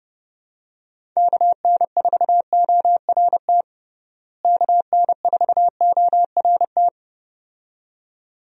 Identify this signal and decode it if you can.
Morse code, heard as a keyed tone, about 30 wpm, 700 Hz. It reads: KN4ORT KN4ORT
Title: USA Call Signs - 6 Characters - Repeated 2x - 30wpm